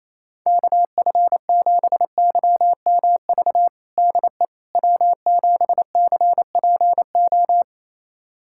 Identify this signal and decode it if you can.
Morse code, heard as a keyed tone, about 27 wpm, 700 Hz. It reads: KF7YMV DE W7CPO
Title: USA Callsign DE Callsign 28wpm